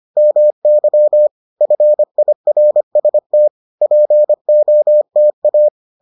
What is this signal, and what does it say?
Morse code, heard as a keyed tone, about 25 wpm, 600 Hz. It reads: MY FIRST POTA